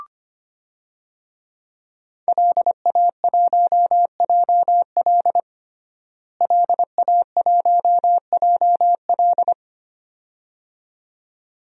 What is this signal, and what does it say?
Morse code, heard as a keyed tone, about 25 wpm, 700 Hz. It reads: LA1JL LA1JL
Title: DXCC-100 Call Signs - 5 Characters - Repeated 2x - 25wpm